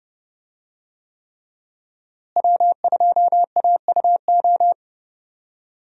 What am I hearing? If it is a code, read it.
Morse code, heard as a keyed tone, about 31 wpm, 700 Hz. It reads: W2AUO
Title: USA Call Signs - 5 Characters - Repeated 3x - 30wpm